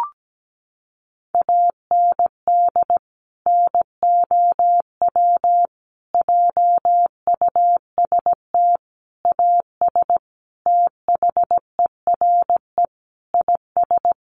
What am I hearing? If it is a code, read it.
Morse code, heard as a keyed tone, about 17 wpm, 700 Hz. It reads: AND NOW JUST AS THERE IS
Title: Sentences from Top 1200 Words 17wpm